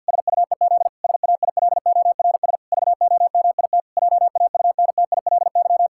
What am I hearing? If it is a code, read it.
Morse code, heard as a keyed tone, about 52 wpm, 700 Hz. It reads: SUEZ SAILORS VOMIT JAUNTILY
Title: Pangram Challenge v2 - 50wpm